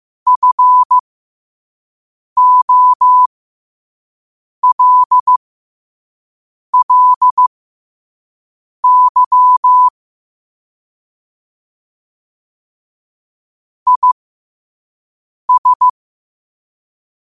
Morse code, heard as a keyed tone, about 15 wpm, spaced as for 5 wpm, 1000 Hz. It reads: FOLLY IS
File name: qotdfiles / 5wpm